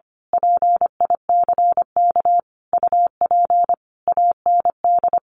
Morse code, heard as a keyed tone, about 25 wpm, 700 Hz. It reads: PICK UP AND
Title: A Sound Of Thunder 25wpm